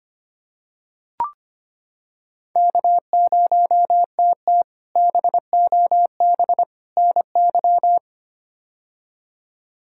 Morse code, heard as a keyed tone, about 25 wpm, 700 Hz. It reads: K0TT BOB NY
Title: K1USN SST QSO Practice - 25wpm